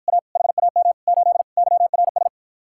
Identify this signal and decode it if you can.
Morse code, heard as a keyed tone, about 52 wpm, 700 Hz. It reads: A HAM 8 YRS